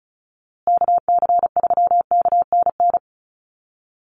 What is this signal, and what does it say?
Morse code, heard as a keyed tone, about 34 wpm, 700 Hz. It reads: KC3KNN
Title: USA Call Signs Repeated 3x 35wpm